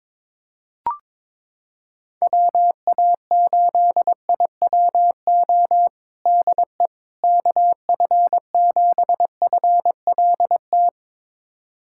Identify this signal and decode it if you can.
Morse code, heard as a keyed tone, about 22 wpm, 700 Hz. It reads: WA8IWO DE KF7FLT
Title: USA Callsign DE Callsign 22wpm